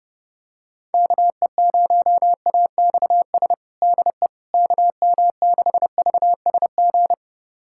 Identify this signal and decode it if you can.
Morse code, heard as a keyed tone, about 32 wpm, 700 Hz. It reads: KE0AXS DE KM6VSG
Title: USA Callsign DE Callsign 30wpm